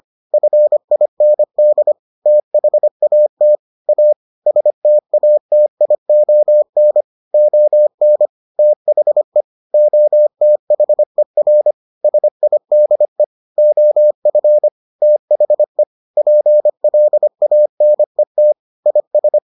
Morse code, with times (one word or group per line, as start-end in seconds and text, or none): 0.3-1.9 FIND
2.3-3.6 THAT
3.9-4.1 A
4.5-7.0 STATION
7.3-8.3 ON
8.6-9.4 THE
9.7-11.7 OTHER
12.0-13.2 SIDE
13.6-14.7 OF
15.0-15.8 THE
16.2-18.5 PLANET
18.9-19.4 IS